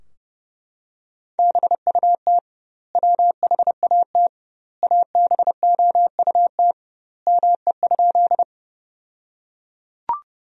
Morse code, heard as a keyed tone, about 30 wpm, 700 Hz. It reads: BUT WHAT ABOUT ME?
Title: 2x Word-Spacing - Sentences from Top 200 Words - 30wpm